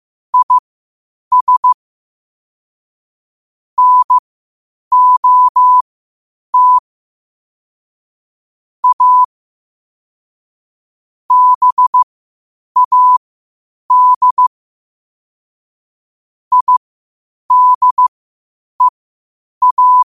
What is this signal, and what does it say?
Morse code, heard as a keyed tone, about 15 wpm, spaced as for 8 wpm, 1000 Hz. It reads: IS NOT A BAD IDEA
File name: qotdfiles / 8wpm